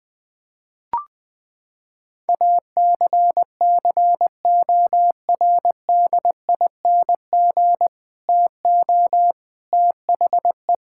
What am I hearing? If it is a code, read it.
Morse code, heard as a keyed tone, about 21 wpm, 700 Hz. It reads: ACCORDING TO THE
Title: Sentences with 2-Word Phrases - 20wpm